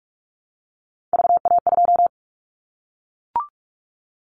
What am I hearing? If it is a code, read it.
Morse code, heard as a keyed tone, about 44 wpm, 700 Hz. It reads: VA2
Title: Callsign Prefixes for Top 25 DXCC - Repeated 3x - 45wpm